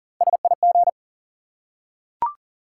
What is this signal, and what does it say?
Morse code, heard as a keyed tone, about 37 wpm, 700 Hz. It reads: SIG
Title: Top 90 QSO Elements 40wpm